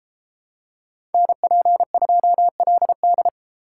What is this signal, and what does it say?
Morse code, heard as a keyed tone, about 30 wpm, 700 Hz. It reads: NP2LD